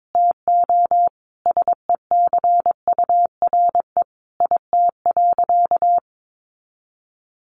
Morse code, heard as a keyed tone, about 22 wpm, 700 Hz. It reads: TO SECURE IT.
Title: Sentences from Top 1200 Words 22wpm